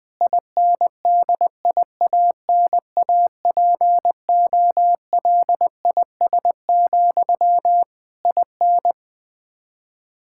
Morse code, heard as a keyed tone, about 20 wpm, 700 Hz. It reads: INDIANAPOLIS, IN
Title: K1USN SST QSO Practice 20wpm